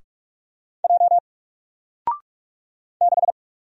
Morse code, heard as a keyed tone, about 44 wpm, 700 Hz. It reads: J 6
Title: Single Character (Letter:Number:Punctuation:Prosign) Alt 45wpm